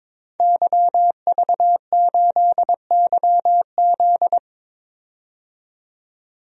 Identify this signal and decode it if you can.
Morse code, heard as a keyed tone, about 21 wpm, 700 Hz. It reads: YV8YZ
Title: DXCC-100 Call Signs - 5 Characters - 22wpm